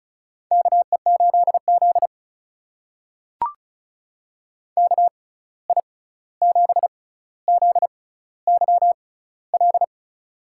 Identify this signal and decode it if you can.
Morse code, heard as a keyed tone, about 36 wpm, spaced as for 16 wpm, 700 Hz. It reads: KE8Z KI7ZYL
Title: USA Call Signs - 6x Character Spacing - 35wpm